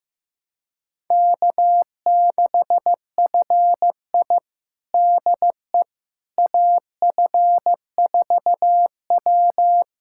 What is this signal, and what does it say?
Morse code, heard as a keyed tone, about 15 wpm, 700 Hz. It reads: K6FI DE AF4W
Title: USA Callsign DE Callsign - 4 Characters - 15wpm